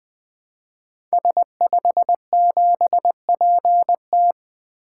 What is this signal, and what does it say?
Morse code, heard as a keyed tone, about 21 wpm, 700 Hz. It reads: S57PT
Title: Code Groups - 5 Letter:Numbers - 20wpm